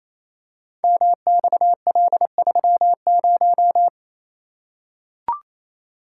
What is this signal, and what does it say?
Morse code, heard as a keyed tone, about 27 wpm, 700 Hz. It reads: MXL30
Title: Code Groups - 5 Letter:Numbers - 28wpm